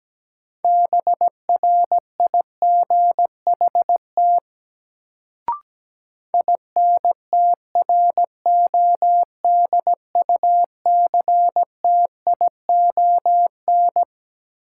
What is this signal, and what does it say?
Morse code, heard as a keyed tone, about 17 wpm, 700 Hz. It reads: BRIGHT INTRODUCTION
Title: Sentences from Top 1000 Words - New Word Drill - 17wpm